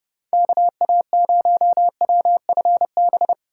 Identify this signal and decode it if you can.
Morse code, heard as a keyed tone, about 30 wpm, 700 Hz. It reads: KA0WFB